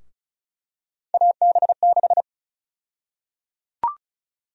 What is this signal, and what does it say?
Morse code, heard as a keyed tone, about 34 wpm, 700 Hz. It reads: AB6
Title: USA Call Sign Prefixes - 35wpm